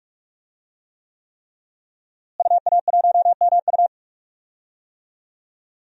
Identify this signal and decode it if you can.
Morse code, heard as a keyed tone, about 44 wpm, 700 Hz. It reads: UA1MU
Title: DXCC-100 Call Signs - 5 Characters - Repeated 3x - 45wpm